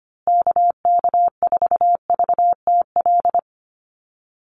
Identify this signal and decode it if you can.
Morse code, heard as a keyed tone, about 25 wpm, 700 Hz. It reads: KK4VTL
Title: USA Call Signs Repeated 2x 25wpm